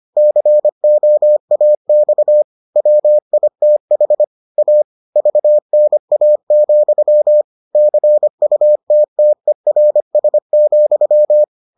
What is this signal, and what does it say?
Morse code, heard as a keyed tone, about 25 wpm, 600 Hz. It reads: COAX WITH A VNA, CUTTERS,